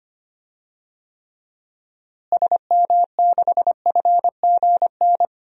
Morse code, heard as a keyed tone, about 25 wpm, 700 Hz. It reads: SM6FGN
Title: DXCC-100 Call Signs - 6 Characters - Repeated 3x - 25wpm